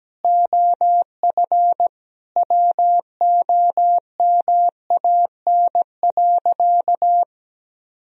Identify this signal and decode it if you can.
Morse code, heard as a keyed tone, about 17 wpm, 700 Hz. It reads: OF WOMAN.
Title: Sentences from Top 1000 Words - Review Part 3 - 17wpm